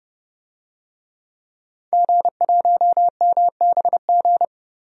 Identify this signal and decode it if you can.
Morse code, heard as a keyed tone, about 30 wpm, 700 Hz. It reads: G1MBG
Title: DXCC-100 Call Signs - 5 Characters - Repeated 3x - 30wpm